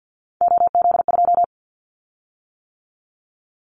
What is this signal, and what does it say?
Morse code, heard as a keyed tone, about 50 wpm, 700 Hz. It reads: OZ2
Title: Callsign Prefixes for Top 50 DXCC - 50wpm